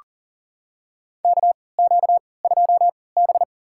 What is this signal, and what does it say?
Morse code, heard as a keyed tone, about 40 wpm, spaced as for 23 wpm, 700 Hz. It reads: KQ2B
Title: USA Call Signs - 4 Characters - 3x Character Spacing - 40wpm